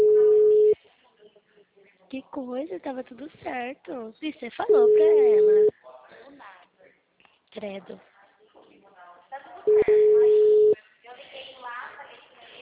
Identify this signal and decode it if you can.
Ringback tone (Europe, 425 Hz), with speech.
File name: out- -2002- - - .8807